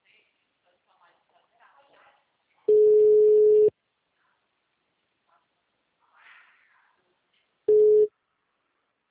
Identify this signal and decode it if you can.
Ringback tone (Europe, 425 Hz), with speech.